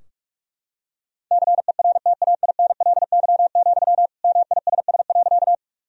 Morse code, heard as a keyed tone, about 46 wpm, 700 Hz. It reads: CERTAINLY, MISS.
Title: Sentences from Top 1000 Words - Review Part 3 - 45wpm